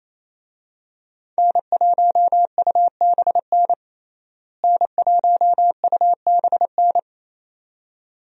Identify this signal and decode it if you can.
Morse code, heard as a keyed tone, about 27 wpm, 700 Hz. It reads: N1UBN N1UBN
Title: USA Call Signs - 5 Characters - Repeated 2x - 28wpm